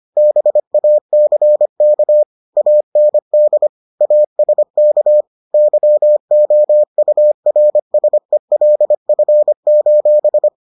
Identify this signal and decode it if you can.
Morse code, heard as a keyed tone, about 25 wpm, 600 Hz. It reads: BACK AND ASK YOURSELF: